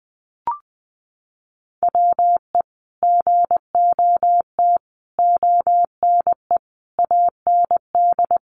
Morse code, heard as a keyed tone, about 19 wpm, 700 Hz. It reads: WE GOT ONE AND
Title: Sentences from Top 300 Words 20wpm